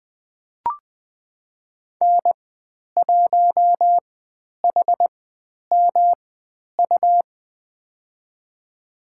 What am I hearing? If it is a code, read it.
Morse code, heard as a keyed tone, about 20 wpm, spaced as for 10 wpm, 700 Hz. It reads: N1HMU
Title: 109 Lesson 18g - Letter m - Call Signs 20wpm